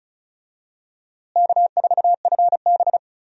Morse code, heard as a keyed tone, about 34 wpm, 700 Hz. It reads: K4FB
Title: USA Call Signs - 4 Characters - Repeated 3x - 35wpm